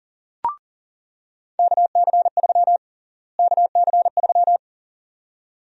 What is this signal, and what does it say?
Morse code, heard as a keyed tone, about 40 wpm, 700 Hz. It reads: KC3 KC3
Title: USA Call Sign Prefixes - Repeated 2x - 40wpm